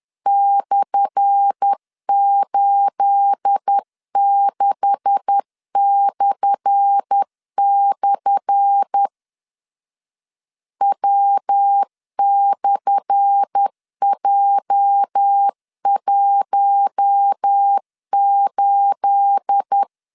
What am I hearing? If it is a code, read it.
Morse code, heard as a keyed tone, about 10 wpm, 800 Hz. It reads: /86// W/J18
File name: LEZIONI / 10 SIGNS / 10A SLASH 10wpm 2808